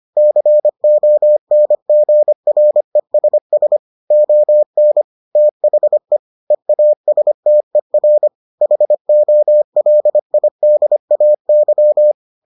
Morse code, heard as a keyed tone, about 25 wpm, 600 Hz. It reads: CONGRESS ON THE EASTER HOLIDAY